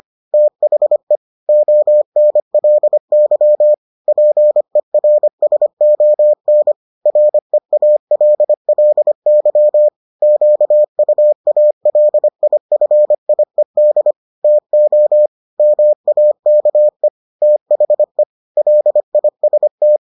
Morse code, heard as a keyed tone, about 25 wpm, 600 Hz. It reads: THE ONLY PERSON REALLY QUALIFIED TO MAKE THE LIST